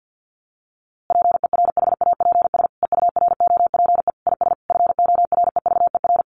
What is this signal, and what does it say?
Morse code, heard as a keyed tone, about 50 wpm, 700 Hz. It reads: PERHAPS EUROPE IS FOREVER
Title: A Sound Of Thunder 50wpm